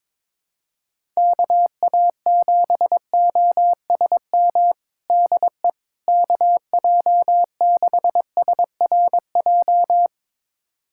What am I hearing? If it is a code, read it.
Morse code, heard as a keyed tone, about 21 wpm, 700 Hz. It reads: KA7OSM DE KJ6SRJ
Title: USA Callsign DE Callsign 22wpm